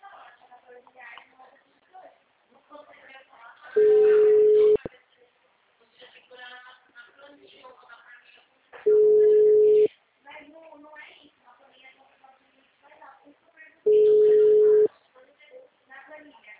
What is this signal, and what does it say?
Ringback tone (Europe, 425 Hz), with speech.